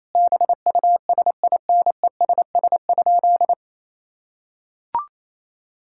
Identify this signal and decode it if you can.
Morse code, heard as a keyed tone, about 27 wpm, 700 Hz. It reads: BUSINESS?
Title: Sentences from Top 1000 Words - Review Part 5 - 28wpm